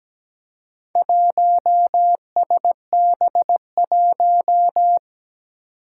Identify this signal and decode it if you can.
Morse code, heard as a keyed tone, about 17 wpm, 700 Hz. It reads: 1SB1
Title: Four Challenging Letter-Number Combinations 17wpm